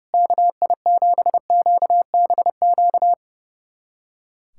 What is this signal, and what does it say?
Morse code, heard as a keyed tone, about 32 wpm, 700 Hz. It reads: KI7QBQ
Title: USA Callsign DE Callsign 30wpm